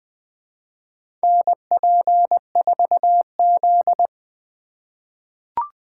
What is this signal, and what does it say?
Morse code, heard as a keyed tone, about 20 wpm, 700 Hz. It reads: NP4Z
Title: USA Call Signs - 2x Character Spacing - 20wpm